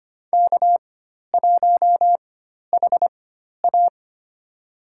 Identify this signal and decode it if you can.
Morse code, heard as a keyed tone, about 25 wpm, spaced as for 12 wpm, 700 Hz. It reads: K1HA
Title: USA Call Signs - 4 Characters - 4x Character Spacing - 25wpm